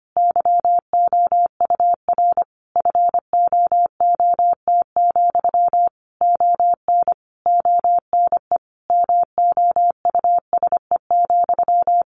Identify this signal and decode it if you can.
Morse code, heard as a keyed tone, about 25 wpm, 700 Hz. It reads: YOUR FOOT, ON ONE MOUSE,
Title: A Sound Of Thunder 25wpm